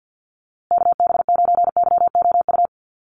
Morse code, heard as a keyed tone, about 50 wpm, 700 Hz. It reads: KB9YOU
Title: USA Call Signs Repeated 2x 50wpm